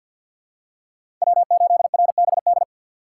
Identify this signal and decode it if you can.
Morse code, heard as a keyed tone, about 47 wpm, 700 Hz. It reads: W8RBD
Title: USA Call Signs - 5 Characters - 4x Character Spacing - 50wpm